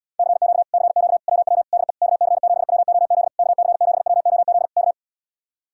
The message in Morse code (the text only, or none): CQ CQ WW DE KC7WOZ KC7WOZ K